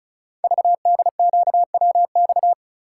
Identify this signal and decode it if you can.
Morse code, heard as a keyed tone, about 36 wpm, 700 Hz. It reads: VDQWX
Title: Code Groups - 5 Letter:Numbers - 35wpm